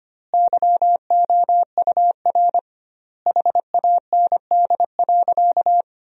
Morse code, heard as a keyed tone, about 25 wpm, 700 Hz. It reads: YOUR HAND.
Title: 2x Word-Spacing - Sentences from Top 400 Words - 25wpm